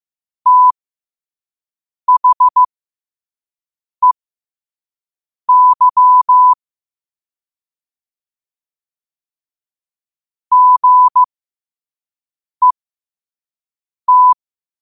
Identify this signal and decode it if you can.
Morse code, heard as a keyed tone, about 15 wpm, spaced as for 5 wpm, 1000 Hz. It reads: THEY GET
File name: qotdfiles / 5wpm